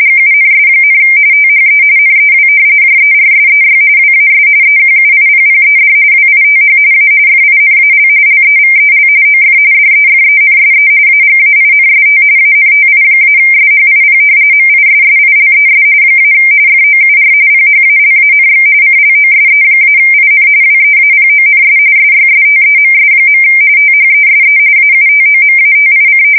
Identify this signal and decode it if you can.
RTTY, about 45.45 baud. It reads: GUNS, THE NEW JERSEY WAS EQUIPPED WITH A FINAL LEVEL OF DEFENSE AGAINST ANTI-SHIP MISSILES - FOUR PHALANX CLOSE-IN WEAPONS SYSTEMS. THESE AUTOMATIC
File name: fsk170 / battleshipnj